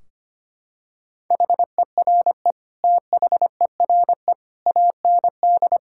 Morse code, heard as a keyed tone, about 25 wpm, 700 Hz. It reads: HERE THERE AND
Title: Sentences from Top 1600 Words 25wpm